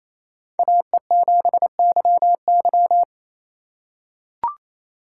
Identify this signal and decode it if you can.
Morse code, heard as a keyed tone, about 28 wpm, 700 Hz. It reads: AE7YY
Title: USA Call Signs - 4x Character Spacing - 28wpm